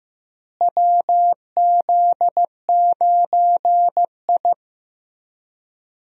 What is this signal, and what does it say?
Morse code, heard as a keyed tone, about 16 wpm, 700 Hz. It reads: WZ9I
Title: USA Call Signs - 4 Characters - Repeated 3x - 15wpm